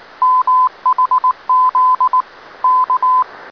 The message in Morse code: MHZ K